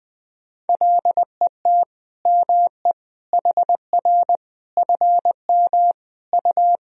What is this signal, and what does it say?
Morse code, heard as a keyed tone, about 20 wpm, 700 Hz. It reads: LET ME HR FM U
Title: Sentences from Top 400 Words and Abbreviations 20wpm